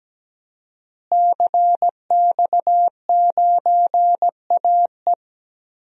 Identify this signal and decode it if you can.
Morse code, heard as a keyed tone, about 17 wpm, 700 Hz. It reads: CX9AE
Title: DXCC-100 Call Signs - 5 Characters - Repeated 2x - 17wpm